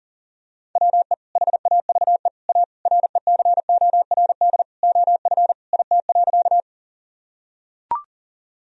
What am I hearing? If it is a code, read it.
Morse code, heard as a keyed tone, about 37 wpm, 700 Hz. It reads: WE HAVE A RECORD OF IT.
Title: Sentences from Top 1000 Words - Review Part 3 - 40wpm